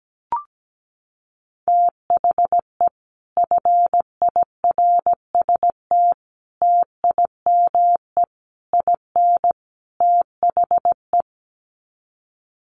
Morse code, heard as a keyed tone, about 17 wpm, 700 Hz. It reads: THE FIRST TIME IN THE
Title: Sets of 5 Words Encore 17wpm